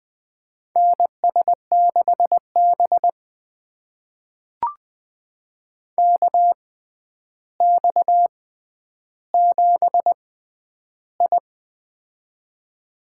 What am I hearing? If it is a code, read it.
Morse code, heard as a keyed tone, about 20 wpm, spaced as for 10 wpm, 700 Hz. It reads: NS6B KX7I
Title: USA Call Signs - 4 Characters - 6x Character Spacing - 20wpm